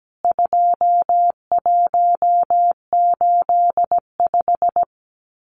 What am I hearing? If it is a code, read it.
Morse code, heard as a keyed tone, about 17 wpm, 700 Hz. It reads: 2185
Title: CWT Exchange 17wpm